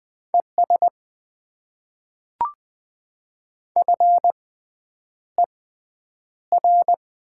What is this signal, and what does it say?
Morse code, heard as a keyed tone, about 20 wpm, spaced as for 9 wpm, 690 Hz. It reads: ES FER
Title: Top 20 QSO Elements - 6x Character Spacing - 20wpm